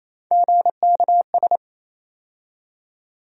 Morse code, heard as a keyed tone, about 28 wpm, 700 Hz. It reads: GKS
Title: USA Call Sign Postfixes - 3 Letters 28wpm